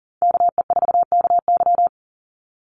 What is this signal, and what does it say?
Morse code, heard as a keyed tone, about 40 wpm, 700 Hz. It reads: KE4KY
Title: Call Signs Repeated 2x - 40wpm